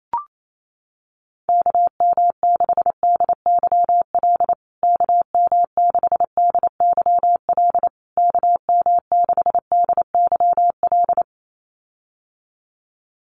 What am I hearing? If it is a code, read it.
Morse code, heard as a keyed tone, about 28 wpm, 700 Hz. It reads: KM6DYL KM6DYL KM6DYL
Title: USA Call Signs Repeated 3x 28wpm